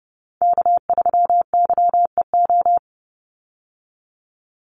Morse code, heard as a keyed tone, about 30 wpm, 700 Hz. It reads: K3YEO
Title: Call Signs Repeated 2x - 30wpm